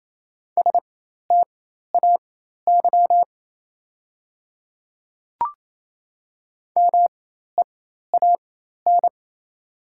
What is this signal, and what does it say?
Morse code, heard as a keyed tone, about 28 wpm, spaced as for 13 wpm, 690 Hz. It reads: STAY MEAN
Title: 4x Character Spacing - Top 500 Words - 28wpm